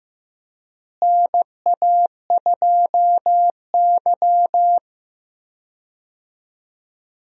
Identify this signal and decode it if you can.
Morse code, heard as a keyed tone, about 15 wpm, 700 Hz. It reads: NA2Y
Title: USA Call Signs - 4 Characters - Repeated 3x - 15wpm